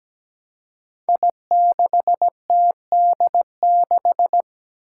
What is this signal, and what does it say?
Morse code, heard as a keyed tone, about 17 wpm, 700 Hz. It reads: I6TD6
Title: Five Challenging Letter-Number Combinations 17wpm